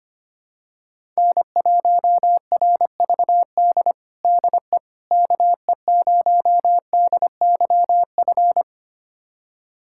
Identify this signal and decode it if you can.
Morse code, heard as a keyed tone, about 25 wpm, 700 Hz. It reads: N1RVD DE KE0DYF
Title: USA Callsign DE Callsign 25wpm